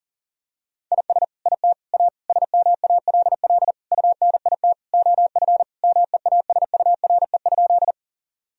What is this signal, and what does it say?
Morse code, heard as a keyed tone, about 40 wpm, 700 Hz. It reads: IS IT A SMALL UNIT OF MEASURE?
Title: Sentences from Top 1000 Words - Review Part 2 - 40wpm